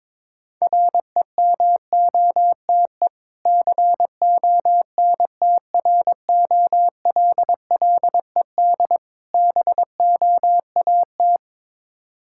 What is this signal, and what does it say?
Morse code, heard as a keyed tone, about 23 wpm, 690 Hz. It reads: REMOTE CONTROLLED BOAT
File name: Difficult Words with Hints - 22wpm